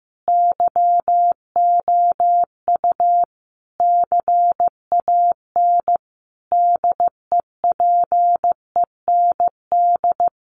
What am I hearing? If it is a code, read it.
Morse code, heard as a keyed tone, about 15 wpm, 700 Hz. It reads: YOU CAN DEPEND
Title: Sentences from Top 1300 Words 15wpm